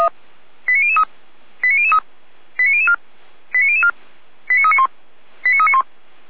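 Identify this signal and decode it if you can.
Five-tone selective call (ZVEI), heard as quick runs of five tones, tones of about 70 ms. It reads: 89003 89003 89004 89004 89391 89391
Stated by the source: ZVEI five-tone selective call 89160 received on SDR, archived by feuerwehr-orlamuende fire brigade server.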